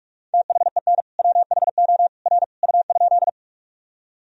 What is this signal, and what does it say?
Morse code, heard as a keyed tone, about 46 wpm, 700 Hz. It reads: THEN WHO R U?
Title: Sentences from Top 100 Words and Abbreviations 45wpm